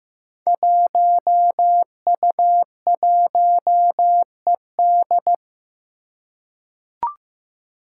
Morse code, heard as a keyed tone, about 15 wpm, 700 Hz. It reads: 1U1ED
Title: Five Challenging Letter-Number Combinations 15wpm